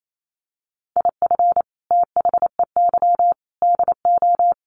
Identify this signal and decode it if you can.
Morse code, heard as a keyed tone, about 28 wpm, 700 Hz. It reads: IF THEY DO
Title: Sets of 4 Words 28wpm